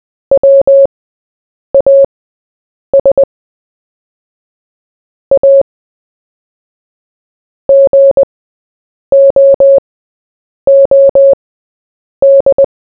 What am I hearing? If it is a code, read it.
Morse code, heard as a keyed tone, about 20 wpm, spaced as for 8 wpm, 550 Hz. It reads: WAS A GOOD
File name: Short-Story-4-8wpm